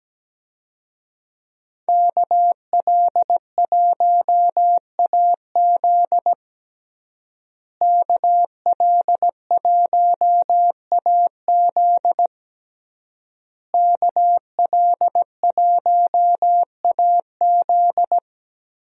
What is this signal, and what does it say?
Morse code, heard as a keyed tone, about 17 wpm, 700 Hz. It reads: KL1AZ KL1AZ KL1AZ
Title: USA Call Signs - 5 Characters - Repeated 3x - 17wpm